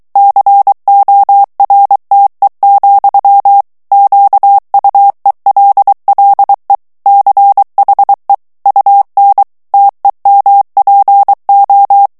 Morse code, heard as a keyed tone, about 23 wpm, 800 Hz. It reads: CORTE, QUELLE CHE UN TEMPO
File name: morse mp3 / ITALIEN texte clair 1 vitesse 22 mots minute